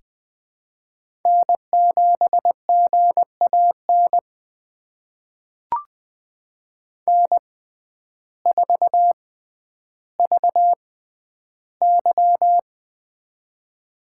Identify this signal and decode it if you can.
Morse code, heard as a keyed tone, about 20 wpm, spaced as for 11 wpm, 700 Hz. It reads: N7GAN N4VY
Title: USA Call Signs - 6x Character Spacing - 20wpm